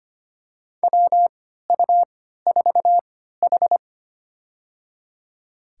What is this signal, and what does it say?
Morse code, heard as a keyed tone, about 25 wpm, spaced as for 14 wpm, 700 Hz. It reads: WU4H
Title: USA Call Signs - 3x Character Spacing - 25wpm